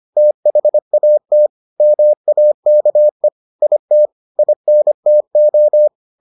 Morse code, heard as a keyed tone, about 25 wpm, 600 Hz. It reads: THAT MAKE IT INTO